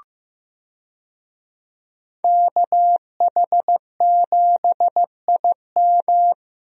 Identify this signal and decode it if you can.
Morse code, heard as a keyed tone, about 15 wpm, 700 Hz. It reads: KH7IM
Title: USA Call Signs - 5 Characters - Repeated 3x - 15wpm